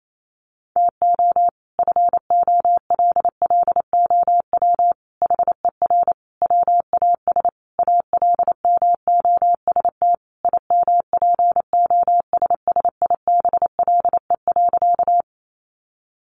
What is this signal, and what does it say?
Morse code, heard as a keyed tone, about 28 wpm, 700 Hz. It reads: TO FOLLOW HER WAS ALMOST IMPOSSIBLE.
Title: Sentences from Top 1100 Words 28wpm